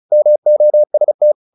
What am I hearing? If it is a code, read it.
Morse code, heard as a keyed tone, about 34 wpm, 600 Hz. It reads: MOST
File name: the old man and the sea / the old man and the sea 1 35wpm